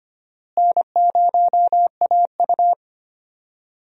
Morse code, heard as a keyed tone, about 25 wpm, 700 Hz. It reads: N0AU
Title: USA Call Signs - 4 Characters - Repeated 3x - 25wpm